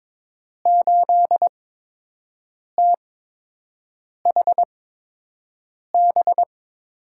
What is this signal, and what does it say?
Morse code, heard as a keyed tone, about 22 wpm, spaced as for 6 wpm, 700 Hz. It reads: 8THB
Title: USA Call Signs - 6 Characters - 8x Character Spacing - 22wpm